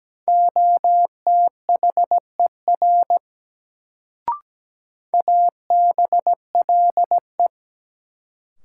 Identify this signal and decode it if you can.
Morse code, heard as a keyed tone, about 17 wpm, 700 Hz. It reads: OTHER ABLE
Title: Sets of 2 Words - New Word Drill - 17wpm